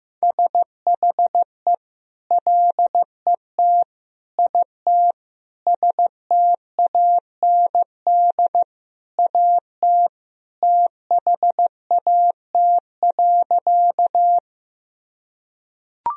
Morse code, heard as a keyed tone, about 15 wpm, 700 Hz. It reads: SHE LET IT STAND AT THAT.
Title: Sentences from Top 1000 Words - Review Part 1 - 15wpm